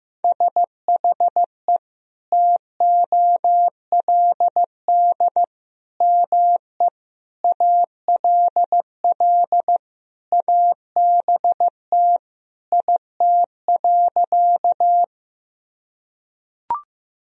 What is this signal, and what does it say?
Morse code, heard as a keyed tone, about 15 wpm, 700 Hz. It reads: SHE TOLD ME ALL ABT IT.
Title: Sentences from Top 400 Words and Abbreviations 15wpm